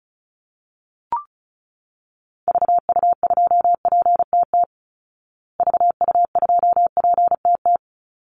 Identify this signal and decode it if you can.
Morse code, heard as a keyed tone, about 34 wpm, 700 Hz. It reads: VU2PTT VU2PTT
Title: Call Signs Repeated 2x - 35wpm